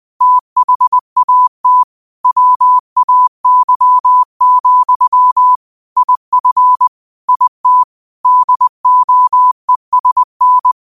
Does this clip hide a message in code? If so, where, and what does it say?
Yes: Morse code from 0.2 to 10.7 s, heard as a keyed tone, about 20 wpm, 1000 Hz. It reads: THAT WAY, IF IT DOESN